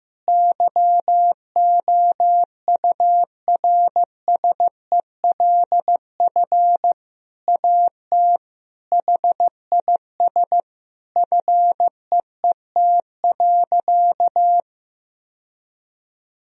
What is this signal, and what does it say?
Morse code, heard as a keyed tone, about 15 wpm, 700 Hz. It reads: YOURSELF AT HIS FEET.
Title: Sentences from Top 1000 Words - Review Part 1 - 15wpm